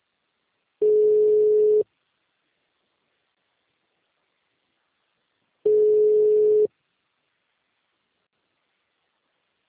Ringback tone (Europe, 425 Hz).